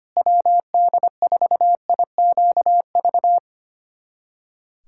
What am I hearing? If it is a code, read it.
Morse code, heard as a keyed tone, about 25 wpm, 700 Hz. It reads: WD4IQV
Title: USA Callsign DE Callsign - 6 Characters - 25wpm